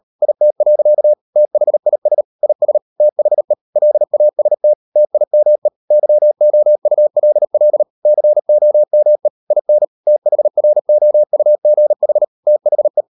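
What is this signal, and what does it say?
Morse code, heard as a keyed tone, about 38 wpm, 600 Hz. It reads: IT. THIS IS THE LAST TIME YOULL COME IN THROUGH THE
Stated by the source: Morse code practice at 38 WPM, sending Ken Follett's novel Never in English, from CW med Gnister.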